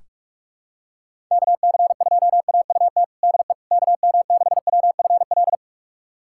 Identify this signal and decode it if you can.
Morse code, heard as a keyed tone, about 46 wpm, 700 Hz. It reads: KC2AUT DE KM6WFL
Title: USA Callsign DE Callsign - 6 Characters - 45wpm